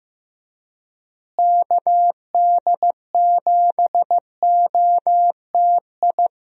Morse code, heard as a keyed tone, about 15 wpm, 700 Hz. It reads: KD7OTI
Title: USA Call Signs - 6 Characters - Repeated 2x - 15wpm